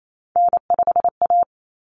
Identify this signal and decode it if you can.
Morse code, heard as a keyed tone, about 28 wpm, 700 Hz. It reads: N5A
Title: Common Cut Numbers - 3-Digit 28wpm